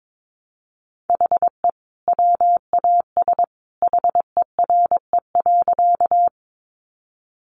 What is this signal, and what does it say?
Morse code, heard as a keyed tone, about 22 wpm, 700 Hz. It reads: HE WAS HERE.
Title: Sentences from Top 200 Words 22wpm